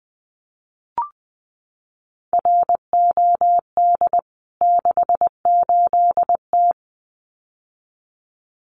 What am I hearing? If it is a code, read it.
Morse code, heard as a keyed tone, about 20 wpm, 700 Hz. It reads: ROD 68T
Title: CWT Exchange 20wpm